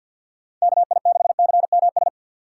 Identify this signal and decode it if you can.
Morse code, heard as a keyed tone, about 54 wpm, 700 Hz. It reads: KI6CMS
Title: USA Call Signs - 6 Characters - 6x Character Spacing - 50wpm